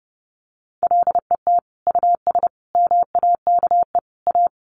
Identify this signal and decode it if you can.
Morse code, heard as a keyed tone, about 30 wpm, 700 Hz. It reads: LET US MAKE A
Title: Sentences from Top 1200 Words 30wpm